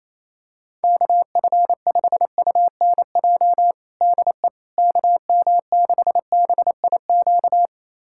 Morse code, heard as a keyed tone, about 28 wpm, 700 Hz. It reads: KF5UNJ DE KM6BIQ
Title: USA Callsign DE Callsign 28wpm